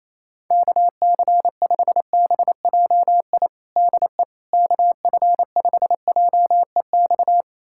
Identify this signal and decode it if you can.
Morse code, heard as a keyed tone, about 27 wpm, 700 Hz. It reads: KC5BJI DE KF5JEX
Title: USA Callsign DE Callsign - 6 Characters - 28wpm